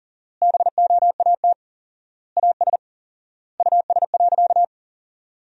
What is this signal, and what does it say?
Morse code, heard as a keyed tone, about 40 wpm, 700 Hz. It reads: BOAT AS US.
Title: 4x Word-Spacing - Sentences from Top 500 Words - 40wpm